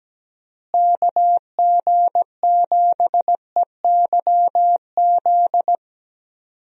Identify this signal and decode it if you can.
Morse code, heard as a keyed tone, about 17 wpm, 700 Hz. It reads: KG7EYZ
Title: USA Call Signs - 6 Characters - Repeated 3x - 17wpm